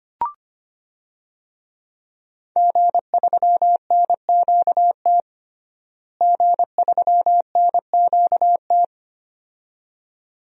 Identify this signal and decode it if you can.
Morse code, heard as a keyed tone, about 25 wpm, 700 Hz. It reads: G3NQT G3NQT
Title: DXCC-100 Call Signs - 5 Characters - Repeated 2x - 25wpm